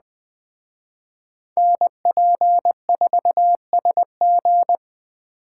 Morse code, heard as a keyed tone, about 20 wpm, 700 Hz. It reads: NP4SG
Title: Speed Racing - USA 2x2 Call Signs 17wpm